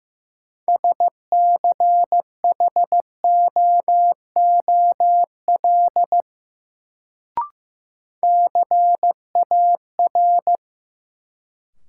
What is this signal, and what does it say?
Morse code, heard as a keyed tone, about 15 wpm, 700 Hz. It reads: SCHOOL CAR
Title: Sentences from Top 300 Words - New Word Drill - 15wpm